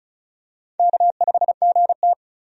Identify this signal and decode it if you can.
Morse code, heard as a keyed tone, about 36 wpm, 700 Hz. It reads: K5GT
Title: USA Call Signs - 4 Characters - 35wpm